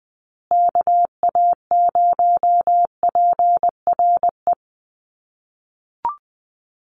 Morse code, heard as a keyed tone, about 20 wpm, 700 Hz. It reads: KA0PRE
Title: USA Call Signs Repeated 3x 20wpm